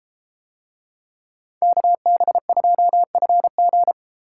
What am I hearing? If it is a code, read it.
Morse code, heard as a keyed tone, about 31 wpm, 700 Hz. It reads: KB2FG